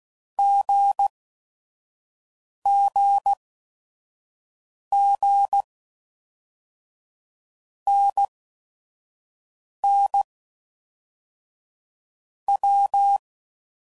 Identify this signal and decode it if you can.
Morse code, heard as a keyed tone, about 16 wpm, 800 Hz. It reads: G G G N N W
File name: Basic Letter Groups By Blocks / Groups@5 / AWUNDG@5 f15